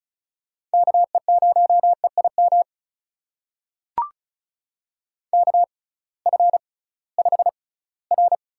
Morse code, heard as a keyed tone, about 36 wpm, spaced as for 20 wpm, 700 Hz. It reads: KE0EIM KF5R
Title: USA Call Signs - 6 Characters - 6x Character Spacing - 35wpm